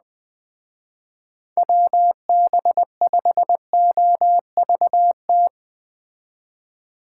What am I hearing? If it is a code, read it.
Morse code, heard as a keyed tone, about 20 wpm, 700 Hz. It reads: WB5OVT